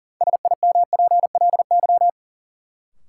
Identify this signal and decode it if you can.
Morse code, heard as a keyed tone, about 40 wpm, 700 Hz. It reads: SIMPLY
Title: Sentences from Top 600 Words - New Word Drill - 40wpm